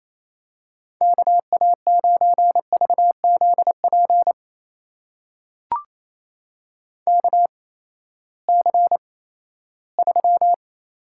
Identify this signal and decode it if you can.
Morse code, heard as a keyed tone, about 28 wpm, spaced as for 16 wpm, 700 Hz. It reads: KA9VZP KC3
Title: USA Call Signs - 6 Characters - 8x Character Spacing - 28wpm